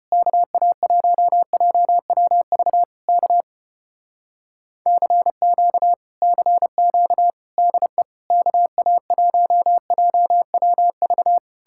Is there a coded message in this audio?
Yes: Morse code from 0.1 to 11.4 s, heard as a keyed tone, about 33 wpm, 700 Hz. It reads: KA1JWV K CQ CQ DE KA1JWV